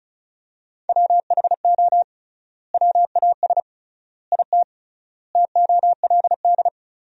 Morse code, heard as a keyed tone, about 34 wpm, 700 Hz. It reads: WHO WAS IT TOLD
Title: 3x Word-Spacing - Sentences from Top 400 Words - 35wpm